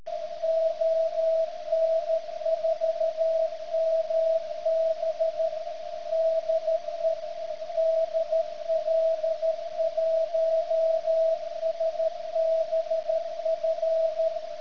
Morse code, heard as a keyed tone, about 13 wpm, 650 Hz, with radio noise. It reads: ON4MB DE DL1SBF
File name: ft8-vs-cw / 90-to-150 CW 100hz